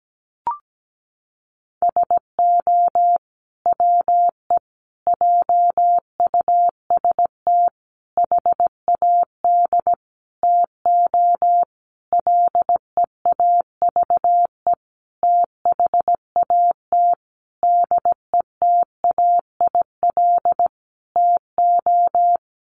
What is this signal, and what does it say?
Morse code, heard as a keyed tone, about 17 wpm, 700 Hz. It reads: SO WE JUST HAD TO LEAVE THAT DETAIL TO
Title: Sentences from Top 1500 Words 17wpm